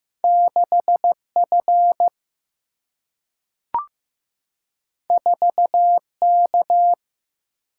Morse code, heard as a keyed tone, about 15 wpm, 700 Hz. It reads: 6 F 4 K
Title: Single-Letter Single-Number 15wpm